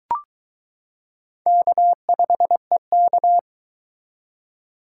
Morse code, heard as a keyed tone, about 22 wpm, 700 Hz. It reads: K5EK